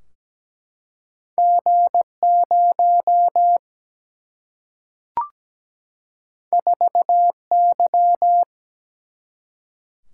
Morse code, heard as a keyed tone, about 17 wpm, 700 Hz. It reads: G 0 4 Y
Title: Single-Letter Single-Number 17wpm